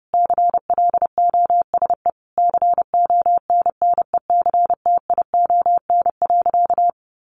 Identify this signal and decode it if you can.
Morse code, heard as a keyed tone, about 30 wpm, 700 Hz. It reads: CLOSE CONNECTION.
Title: Sentences from Top 1300 Words 30wpm